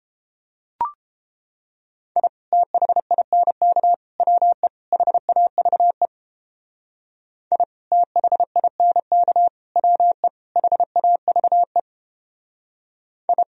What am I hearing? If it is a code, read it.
Morse code, heard as a keyed tone, about 29 wpm, 700 Hz. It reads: I THINK WE HAVE I THINK WE HAVE I